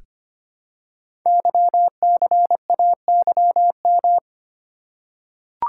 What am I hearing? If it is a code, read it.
Morse code, heard as a keyed tone, about 25 wpm, 700 Hz. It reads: YCAYM
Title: Code Groups - 5 Letters - 25wpm